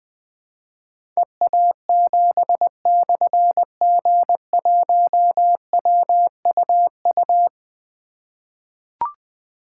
Morse code, heard as a keyed tone, about 21 wpm, 700 Hz. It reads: EA7/G1WUU
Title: Difficult Call Signs 20wpm